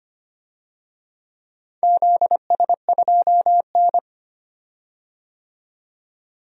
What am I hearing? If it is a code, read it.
Morse code, heard as a keyed tone, about 25 wpm, 700 Hz. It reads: ZS2N
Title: DXCC-100 Call Signs - 4 Characters - 25wpm